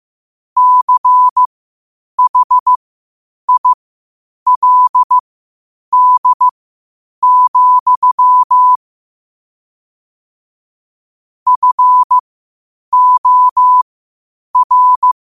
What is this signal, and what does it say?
Morse code, heard as a keyed tone, about 15 wpm, spaced as for 8 wpm, 1000 Hz. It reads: CHILD, FOR
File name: qotdfiles / 8wpm